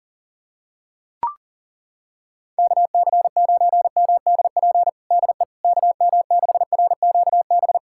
Morse code, heard as a keyed tone, about 40 wpm, 700 Hz. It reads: KC9MDP DE KM6RQB
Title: USA Callsign DE Callsign - 6 Characters - 40wpm